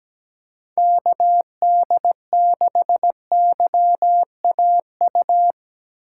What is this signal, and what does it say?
Morse code, heard as a keyed tone, about 17 wpm, 700 Hz. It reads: KD6YAU
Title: USA Call Signs - 6x Character Spacing - 17wpm